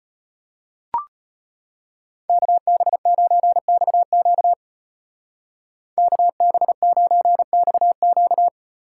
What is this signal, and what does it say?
Morse code, heard as a keyed tone, about 30 wpm, 700 Hz. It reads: KB9XQ KB9XQ
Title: Speed Racing - USA 2x2 Call Signs 25wpm